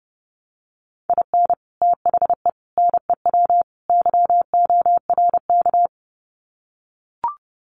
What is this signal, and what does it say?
Morse code, heard as a keyed tone, about 30 wpm, 700 Hz. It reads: IN THE NEW YORK
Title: Sets of 4 Words 30wpm